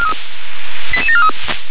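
Five-tone selective call (ZVEI), heard as quick runs of five tones, tones of about 70 ms. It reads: 88743
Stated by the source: ZVEI five-tone selective call received on SDR.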